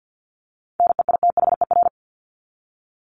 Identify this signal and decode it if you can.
Morse code, heard as a keyed tone, about 50 wpm, 700 Hz. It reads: NEITHER
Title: Top 1501-1750 Words 50wpm